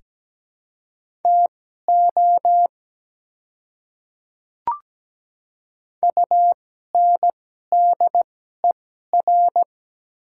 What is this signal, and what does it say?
Morse code, heard as a keyed tone, about 17 wpm, spaced as for 12 wpm, 700 Hz. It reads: TO UNDER
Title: 2x Character Spacing - Top 300 Words - 17wpm